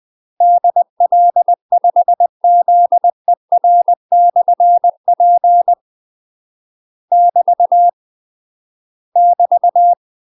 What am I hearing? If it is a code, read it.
Morse code, heard as a keyed tone, about 20 wpm, 700 Hz. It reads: DL5ZER/P <BT> <BT>
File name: UK MASTER QSOs TRAINING/20WPM